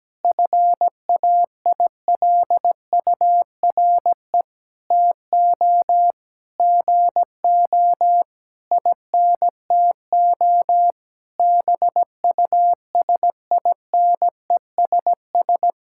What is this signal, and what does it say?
Morse code, heard as a keyed tone, about 17 wpm, 700 Hz. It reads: FAILURE TO GO INTO BUSINESS
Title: Sentences from Top 1000 Words - Review Part 6 - 17wpm